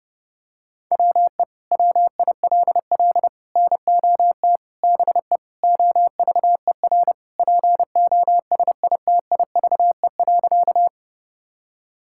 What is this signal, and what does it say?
Morse code, heard as a keyed tone, about 30 wpm, 700 Hz. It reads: WE WILL NOT BE OVER POSITIVE.
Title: Sentences from Top 1000 Words - Review Part 5 - 30wpm